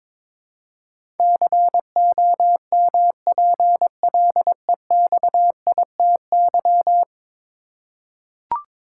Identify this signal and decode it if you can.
Morse code, heard as a keyed tone, about 22 wpm, 700 Hz. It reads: COMPLEXITY
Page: morse-code-ninja-practice-22wpm